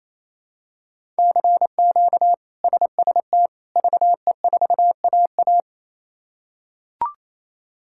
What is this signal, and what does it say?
Morse code, heard as a keyed tone, about 28 wpm, 700 Hz. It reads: CQ SST VE4AA
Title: K1USN SST QSO Practice - 28wpm